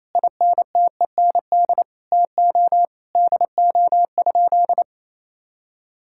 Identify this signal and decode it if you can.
Morse code, heard as a keyed tone, about 28 wpm, 700 Hz. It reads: INTEND TO DO?
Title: Sentences from Top 1600 Words 28wpm